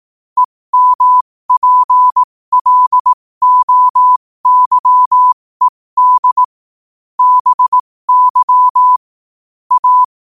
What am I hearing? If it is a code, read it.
Morse code, heard as a keyed tone, about 18 wpm, spaced as for 15 wpm, 1000 Hz. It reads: EMPLOYED BY A
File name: qotdfiles / 15wpm